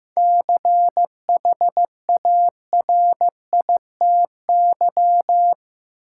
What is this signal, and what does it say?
Morse code, heard as a keyed tone, about 15 wpm, 700 Hz. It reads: CHARITY
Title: Seven Letter Words - 15wpm